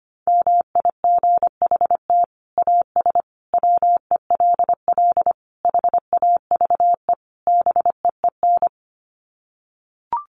Morse code, heard as a keyed tone, about 25 wpm, 700 Hz. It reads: MIGHT AS WELL HAVE BEEN
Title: Sets of 5 Words Encore2 25wpm